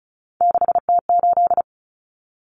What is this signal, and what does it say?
Morse code, heard as a keyed tone, about 34 wpm, 700 Hz. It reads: 6T8
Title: Common Cut Numbers - 3-Digit 35wpm